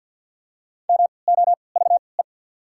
Morse code, heard as a keyed tone, about 54 wpm, spaced as for 29 wpm, 700 Hz. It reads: MOVE
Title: 3x Character Spacing - Top 500 Words - 50wpm